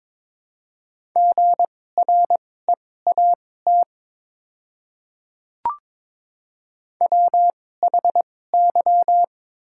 Morse code, heard as a keyed tone, about 22 wpm, spaced as for 16 wpm, 690 Hz. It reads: GREAT WHY
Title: 2x Character Spacing - Top 200 Words - 22wpm